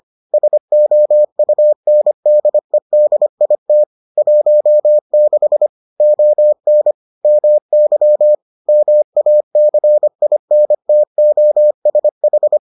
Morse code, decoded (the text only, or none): SOUNDEDIT 16 ON MY MACINTOSH